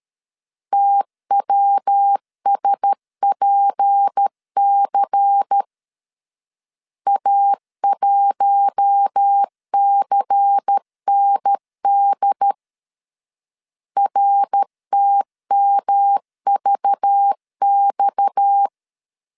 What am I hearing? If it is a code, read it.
Morse code, heard as a keyed tone, about 13 wpm, 800 Hz. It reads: TWSPC A1CND RTMVX
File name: LEZIONI / 6 PKX0 / 6 PKX 12wpm 2024